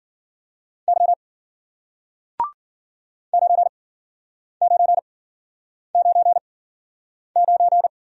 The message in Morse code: Y 9999

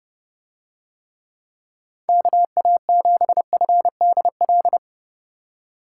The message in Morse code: KA7FDL

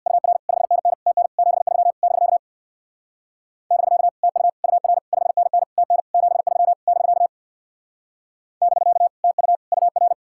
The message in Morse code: UR 5NN TN 73 <BK> <BK> TU UR 5NN TN 73 <BK> <BK> TU UR

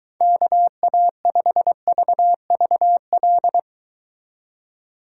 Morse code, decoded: KA5VVL

